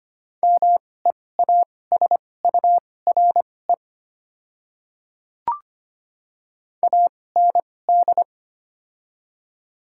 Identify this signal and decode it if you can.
Morse code, heard as a keyed tone, about 25 wpm, spaced as for 18 wpm, 700 Hz. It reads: MEASURE AND
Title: 2x Character Spacing - Top 500 Words - 25wpm